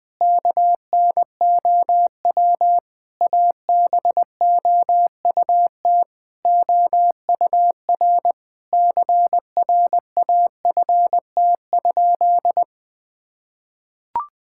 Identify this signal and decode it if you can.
Morse code, heard as a keyed tone, about 20 wpm, 700 Hz. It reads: KNOW ABOUT OUR CRAFT?
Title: Sentences from Top 1000 Words - Review Part 6 - 20wpm